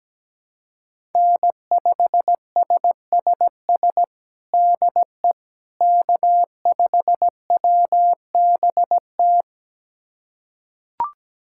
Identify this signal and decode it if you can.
Morse code, heard as a keyed tone, about 17 wpm, 700 Hz. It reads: N5SSS DE K5WBT
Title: USA Callsign DE Callsign - 5 Characters - 17wpm